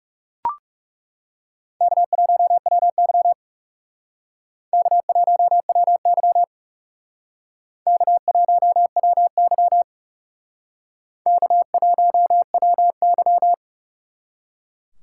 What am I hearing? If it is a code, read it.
Morse code, heard as a keyed tone, about 38 wpm, 700 Hz. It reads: K1WY K1WY K1WY K1WY